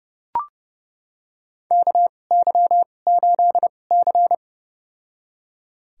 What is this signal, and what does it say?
Morse code, heard as a keyed tone, about 30 wpm, spaced as for 22 wpm, 700 Hz. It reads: KY8C
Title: USA Call Signs - 4 Characters - 2x Character Spacing - 30wpm